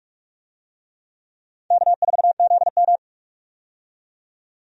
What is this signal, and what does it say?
Morse code, heard as a keyed tone, about 44 wpm, 700 Hz. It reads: K4ZM
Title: USA Call Signs - 4 Characters - 45wpm